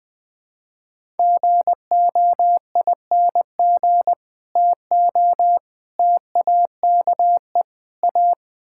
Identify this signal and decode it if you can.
Morse code, heard as a keyed tone, about 20 wpm, 700 Hz. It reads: GOING TO TAKE A
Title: Build A Phrase - 4-Word Phrases - 20wpm